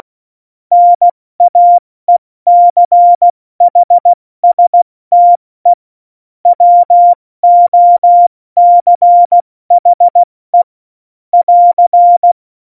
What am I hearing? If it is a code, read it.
Morse code, heard as a keyed tone, about 16 wpm, 700 Hz. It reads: NAECHSTE WOCHE <AR>
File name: Morsekurs / Witze / Witz 02 Juni 2020, 70 F80